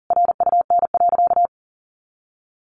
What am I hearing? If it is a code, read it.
Morse code, heard as a keyed tone, about 40 wpm, 700 Hz. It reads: RUN.
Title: Sentences from Top 1500 Words 40wpm